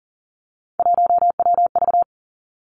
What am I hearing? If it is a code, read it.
Morse code, heard as a keyed tone, about 40 wpm, 700 Hz. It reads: 1WV
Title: Three Challenging Letter-Number Combinations 40wpm